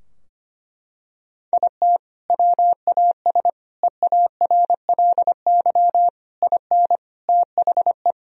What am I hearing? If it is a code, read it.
Morse code, heard as a keyed tone, about 25 wpm, 700 Hz. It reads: IT WAS EARLY IN THE
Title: Sentences from Top 1000 Words - Review Part 2 - 25wpm